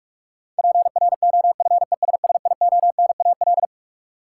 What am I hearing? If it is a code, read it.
Morse code, heard as a keyed tone, about 42 wpm, 700 Hz. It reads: PROFESSIONAL
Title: Sentences from Top 1000 Words - Review Part 4 - New Word Drill - 45wpm